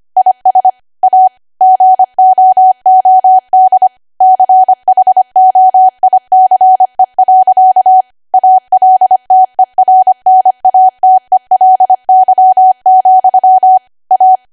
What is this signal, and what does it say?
Morse code, heard as a keyed tone, about 25 wpm, 750 Hz. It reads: IS A GOOD CHOICE. ALTERNATELY, A